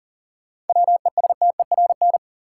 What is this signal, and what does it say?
Morse code, heard as a keyed tone, about 40 wpm, 700 Hz. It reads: WESTERN
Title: Sentences from Top 1000 Words - Review Part 7 - New Word Drill - 40wpm